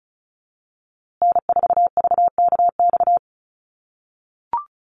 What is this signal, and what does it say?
Morse code, heard as a keyed tone, about 34 wpm, 700 Hz. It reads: N4VKX
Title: USA Call Signs Repeated 3x 35wpm